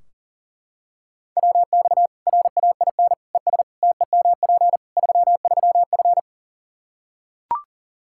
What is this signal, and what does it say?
Morse code, heard as a keyed tone, about 40 wpm, 700 Hz. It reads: WX RAIN ES TEMP 33F
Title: Practice QSO - Exchange 2 - Info - 40wpm